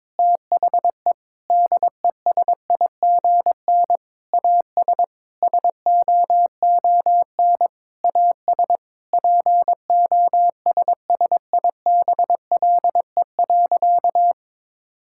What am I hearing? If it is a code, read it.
Morse code, heard as a keyed tone, about 21 wpm, 700 Hz. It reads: THE DESIGN AS SOON AS POSSIBLE.
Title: Sentences from Top 1000 Words - Review Part 5 - 22wpm